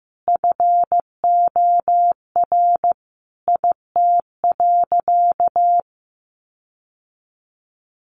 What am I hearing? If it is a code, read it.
Morse code, heard as a keyed tone, about 15 wpm, 700 Hz. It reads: FOR IT.
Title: Sentences from Top 1300 Words 15wpm